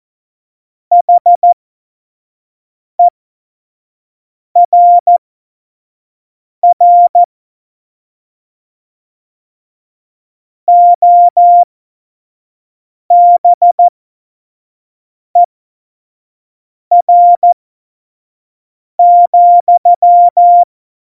Morse code, heard as a keyed tone, about 14 wpm, spaced as for 5 wpm, 700 Hz. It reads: HERR OBER,
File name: Morsekurs / Witze / Witz 26 Mai 2020, 25 F70